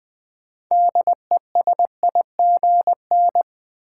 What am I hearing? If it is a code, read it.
Morse code, heard as a keyed tone, about 21 wpm, 700 Hz. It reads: DESIGN
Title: Sentences from Top 800 Words - New Word Drill - 20wpm